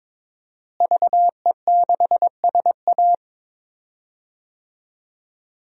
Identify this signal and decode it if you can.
Morse code, heard as a keyed tone, about 22 wpm, 700 Hz. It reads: VE6SA
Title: DXCC-100 Call Signs - 5 Characters - Repeated 3x - 22wpm